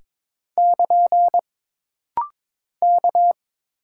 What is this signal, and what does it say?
Morse code, heard as a keyed tone, about 22 wpm, 700 Hz. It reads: <KN> K
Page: This rapid-fire timing is geared to push your ICR (Instant Character Recognition) skills.